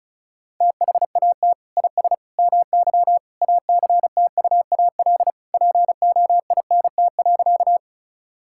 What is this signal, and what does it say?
Morse code, heard as a keyed tone, about 34 wpm, 700 Hz. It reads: THAT IS MY ACTUAL POINT.
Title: Sentences from Top 1000 Words - Review Part 5 - 35wpm